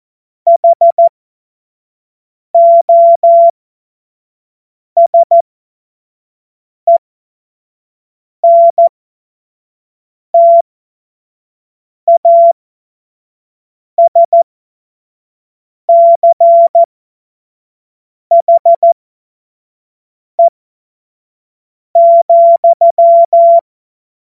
Morse code, heard as a keyed tone, about 14 wpm, spaced as for 5 wpm, 680 Hz. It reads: HOSENTASCHE,
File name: Morsekurs / Witze / Witz 24.Oktober 2023 25 F70